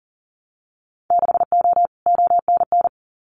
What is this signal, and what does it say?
Morse code, heard as a keyed tone, about 40 wpm, 700 Hz. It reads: 6O ONN
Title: Winter Field Day Exchanges Encore 40wpm